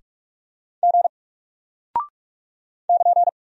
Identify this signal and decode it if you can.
Morse code, heard as a keyed tone, about 44 wpm, 700 Hz. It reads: G <KN>
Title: Single Character (Letter:Number:Punctuation:Prosign) Alt 45wpm